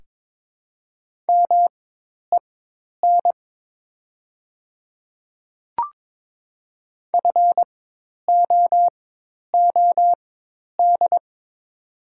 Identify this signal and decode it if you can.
Morse code, heard as a keyed tone, about 22 wpm, spaced as for 10 wpm, 700 Hz. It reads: MEN FOOD
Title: 4x Character Spacing - Top 300 Words - 22wpm